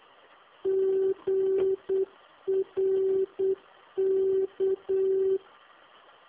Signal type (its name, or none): Morse code